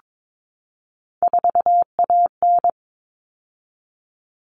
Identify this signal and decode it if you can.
Morse code, heard as a keyed tone, about 22 wpm, 700 Hz. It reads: 4AN
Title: All Cut Numbers - 3-Digit 22wpm